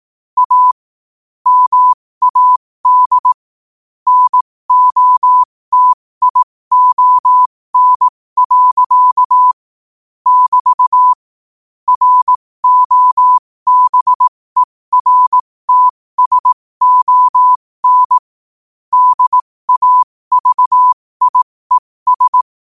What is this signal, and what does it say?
Morse code, heard as a keyed tone, about 18 wpm, spaced as for 15 wpm, 1000 Hz. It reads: A MAD NOTION. <BT> ROBERTSON DAVIES
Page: 15 wpm morse code quotes